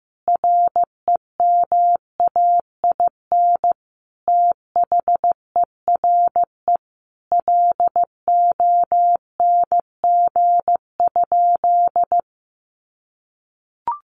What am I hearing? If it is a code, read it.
Morse code, heard as a keyed tone, about 15 wpm, 700 Hz. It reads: REMAIN THERE LONG?
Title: Sentences from Top 1200 Words 15wpm